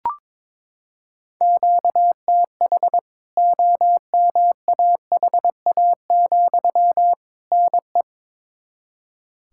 Morse code, heard as a keyed tone, about 22 wpm, 700 Hz. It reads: QTH OMAHA, NE
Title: K1USN SST QSO Practice 22wpm